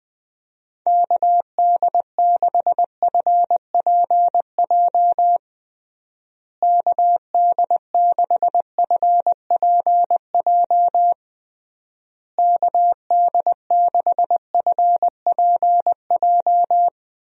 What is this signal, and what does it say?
Morse code, heard as a keyed tone, about 20 wpm, 700 Hz. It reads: KD6FPJ KD6FPJ KD6FPJ
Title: USA Call Signs - 6 Characters - Repeated 3x - 20wpm